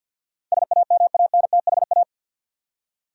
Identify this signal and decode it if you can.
Morse code, heard as a keyed tone, about 47 wpm, 700 Hz. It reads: SAMANTHA
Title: USA Common Names - 50wpm